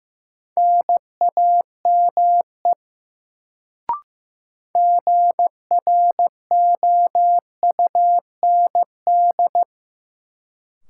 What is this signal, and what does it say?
Morse code, heard as a keyed tone, about 15 wpm, 700 Hz. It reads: NAME GROUND
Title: Sets of 3 Words - New Word Drill - 15wpm